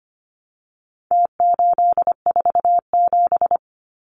Morse code, heard as a keyed tone, about 25 wpm, 700 Hz. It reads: T847
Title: Numbers - Contest Style - 4-Digit 25wpm